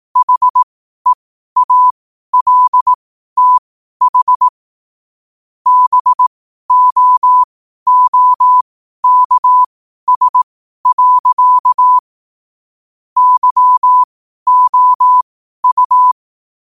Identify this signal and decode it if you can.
Morse code, heard as a keyed tone, about 18 wpm, spaced as for 12 wpm, 1000 Hz. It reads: HEALTH BOOKS. YOU
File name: qotdfiles / 12wpm